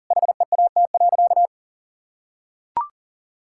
Morse code, heard as a keyed tone, about 40 wpm, 700 Hz. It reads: HEAT.
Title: Sentences from Top 1000 Words - Review Part 6 - 40wpm